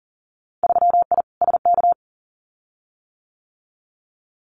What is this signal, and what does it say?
Morse code, heard as a keyed tone, about 40 wpm, 700 Hz. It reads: 3I SK
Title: Winter Field Day Exchanges Encore 40wpm